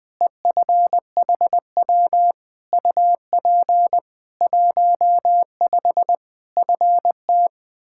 Morse code, heard as a keyed tone, about 20 wpm, 700 Hz. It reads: EFHW UP 15 FT
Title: K1USN SST QSO Practice 20wpm